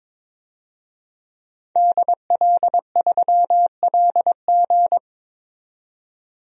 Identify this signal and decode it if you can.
Morse code, heard as a keyed tone, about 22 wpm, 700 Hz. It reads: DL3LG
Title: DXCC-100 Call Signs - 5 Characters - 22wpm